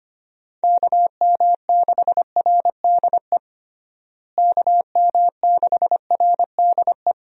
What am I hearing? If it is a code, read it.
Morse code, heard as a keyed tone, about 25 wpm, 700 Hz. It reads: KM6RDE KM6RDE
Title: USA Call Signs - 6 Characters - Repeated 2x - 25wpm